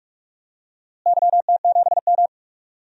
Morse code, heard as a keyed tone, about 42 wpm, 700 Hz. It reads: YT7M
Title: DXCC-100 Call Signs - 4 Characters - Repeated 3x - 45wpm